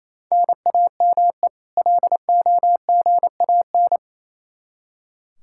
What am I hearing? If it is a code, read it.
Morse code, heard as a keyed tone, about 27 wpm, 700 Hz. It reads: NAME LOGAN
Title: Practice QSO - Exchange 1 - Beginning - 28wpm